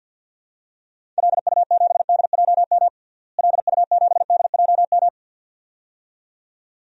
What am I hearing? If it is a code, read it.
Morse code, heard as a keyed tone, about 48 wpm, 700 Hz. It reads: LU7DJM LU7DJM
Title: DXCC-100 Call Signs - 6 Characters - Repeated 2x - 50wpm